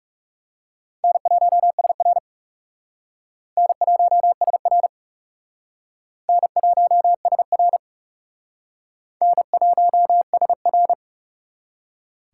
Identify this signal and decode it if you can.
Morse code, heard as a keyed tone, about 38 wpm, 700 Hz. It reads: N1SR N1SR N1SR N1SR
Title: Speed Racing - USA 1x2 Call Signs 30wpm